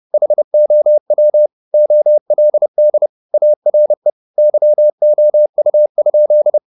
Morse code, heard as a keyed tone, about 30 wpm, 600 Hz. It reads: HOW OLD ARE YOU?
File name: the old man and the sea / the old man and the sea 1 30wpm